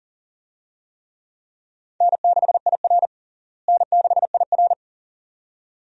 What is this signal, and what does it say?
Morse code, heard as a keyed tone, about 40 wpm, 700 Hz. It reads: N6IR N6IR
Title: USA Call Signs - 4 Characters - Repeated 2x - 40wpm